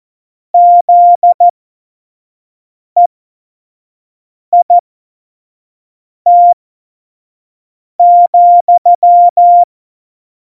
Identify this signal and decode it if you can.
Morse code, heard as a keyed tone, about 14 wpm, spaced as for 5 wpm, 700 Hz. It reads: ZEIT,
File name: Morsekurs / Witze / Witz 14. Januar 2020, 25 F70